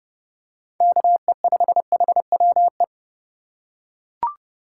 Morse code, heard as a keyed tone, about 30 wpm, 700 Hz. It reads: KE5HWE
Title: USA Call Signs - 6 Characters - 2x Character Spacing - 30wpm